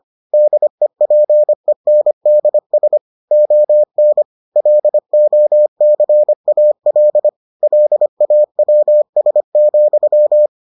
Morse code, heard as a keyed tone, about 25 wpm, 600 Hz. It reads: DEPENDS ON LOCAL LAWS,